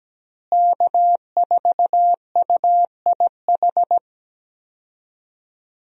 Morse code, heard as a keyed tone, about 17 wpm, 700 Hz. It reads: K4UIH
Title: USA Call Signs - 5 Characters - Repeated 2x - 17wpm